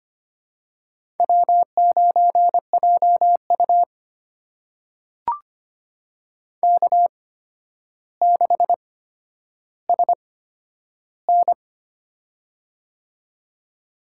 Morse code, heard as a keyed tone, about 25 wpm, spaced as for 11 wpm, 700 Hz. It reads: W9JU K6SN
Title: USA Call Signs - 4 Characters - 8x Character Spacing - 25wpm